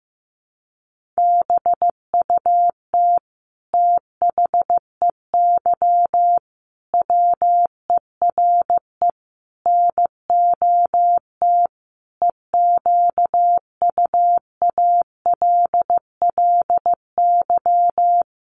Morse code, heard as a keyed tone, about 15 wpm, 700 Hz. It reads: BUT THEY WERE NOT EQUALLY
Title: Sentences from Top 1400 Words 15wpm